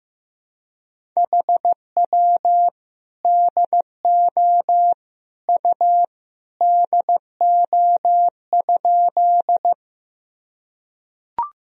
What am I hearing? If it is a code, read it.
Morse code, heard as a keyed tone, about 15 wpm, 700 Hz. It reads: HW DO U DO?
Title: Sentences from Top 100 Words and Abbreviations 15wpm